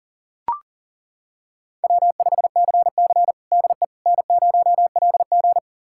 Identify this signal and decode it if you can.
Morse code, heard as a keyed tone, about 40 wpm, 700 Hz. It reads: W5CC DE N0LG
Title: USA Callsign DE Callsign - 4 Characters - 40wpm